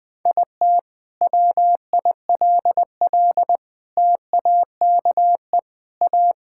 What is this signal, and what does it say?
Morse code, heard as a keyed tone, about 20 wpm, 700 Hz. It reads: IT WILL TAKE A
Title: Sentences from Top 1000 Words - Review Part 5 - 20wpm